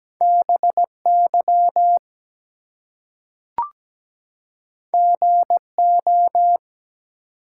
Morse code, heard as a keyed tone, about 17 wpm, 700 Hz. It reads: BY GO
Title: Two Letter Words - 17wpm